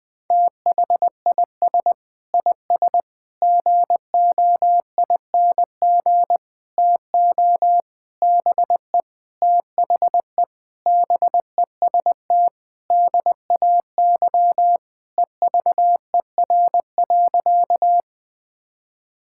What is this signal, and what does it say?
Morse code, heard as a keyed tone, about 20 wpm, 700 Hz. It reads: THIS IS GOING TO BE THE BEST DAY EVER.
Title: Sentences with 5-Word Phrases - 20wpm